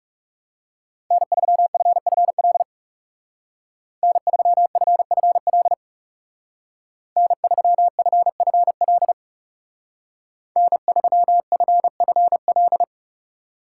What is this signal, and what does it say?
Morse code, heard as a keyed tone, about 35 wpm, 700 Hz. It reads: N3FFL N3FFL N3FFL N3FFL
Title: Speed Racing - USA 1x3 Call Signs 30wpm